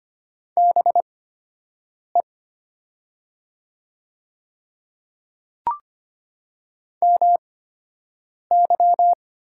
Morse code, heard as a keyed tone, about 25 wpm, spaced as for 7 wpm, 700 Hz. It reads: BE MY
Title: 8x Character Spacing - Top 300 Words - 25wpm